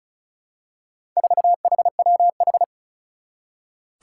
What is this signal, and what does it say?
Morse code, heard as a keyed tone, about 36 wpm, 700 Hz. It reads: 4HWH
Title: Four Challenging Letter-Number Combinations 35wpm